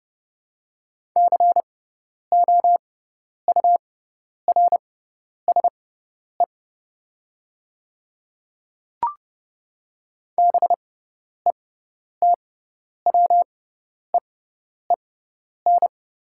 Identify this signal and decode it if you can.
Morse code, heard as a keyed tone, about 30 wpm, spaced as for 10 wpm, 690 Hz. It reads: COURSE BETWEEN
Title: 6x Character Spacing - Top 500 Words - 30wpm